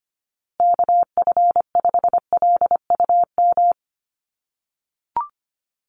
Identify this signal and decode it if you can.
Morse code, heard as a keyed tone, about 25 wpm, 700 Hz. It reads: KF5LUM
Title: USA Call Signs Repeated 2x 25wpm